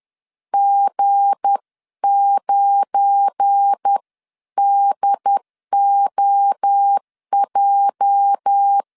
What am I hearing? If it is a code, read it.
Morse code, heard as a keyed tone, about 10 wpm, 800 Hz. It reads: G9DOJ
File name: LEZIONI / 8 YJ45 / 8 YJ 10wpm 2651